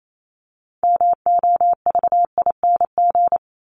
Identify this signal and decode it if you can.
Morse code, heard as a keyed tone, about 28 wpm, 700 Hz. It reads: MOVING
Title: ING Suffix - Top 100 Words with Base Words 28wpm